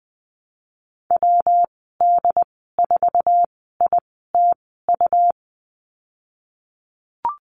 Morse code, heard as a keyed tone, about 20 wpm, spaced as for 15 wpm, 700 Hz. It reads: WD4ITU
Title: 245 Lesson 39g - Letter z - Call Signs 20wpm